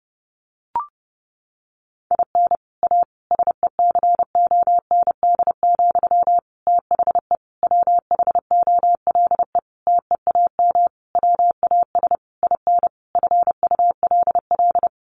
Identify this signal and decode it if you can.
Morse code, heard as a keyed tone, about 30 wpm, 700 Hz. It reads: IN A SECOND, THE WHOLE TEAM WAS IN FULL
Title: Sentences from Top 1500 Words 30wpm